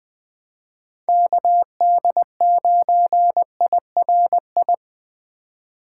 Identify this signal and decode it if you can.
Morse code, heard as a keyed tone, about 21 wpm, 700 Hz. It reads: KD9IRI
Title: USA Call Signs - 6 Characters - 2x Character Spacing - 20wpm